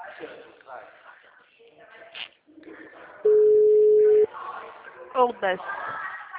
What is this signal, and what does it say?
Ringback tone (Europe, 425 Hz), with speech.